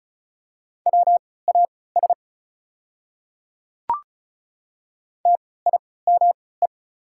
Morse code, heard as a keyed tone, about 36 wpm, spaced as for 20 wpm, 690 Hz. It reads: WAS TIME
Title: 3x Character Spacing - Top 100 Words - 35wpm